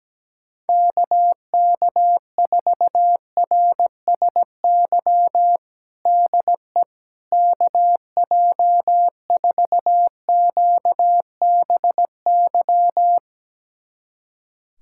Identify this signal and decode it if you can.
Morse code, heard as a keyed tone, about 17 wpm, 700 Hz. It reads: KK4RSY DE KJ4QBY
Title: USA Callsign DE Callsign - 6 Characters - 17wpm